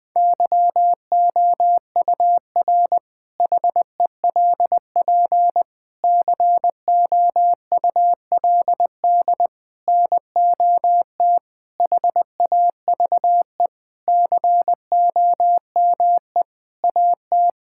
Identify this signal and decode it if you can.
Morse code, heard as a keyed tone, about 21 wpm, 700 Hz. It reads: YOUR HELP COULD NOT HAVE COME AT
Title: Sentences with 8-Word Phrases - Spoken Beginning - 20wpm